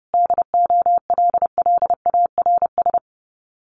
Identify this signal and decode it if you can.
Morse code, heard as a keyed tone, about 30 wpm, 700 Hz. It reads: DOLLARS
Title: Sets of 5 Words Encore 30wpm